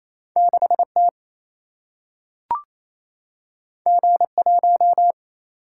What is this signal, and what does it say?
Morse code, heard as a keyed tone, about 28 wpm, 700 Hz. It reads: 6 T G 1
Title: Single-Letter Single-Number 28wpm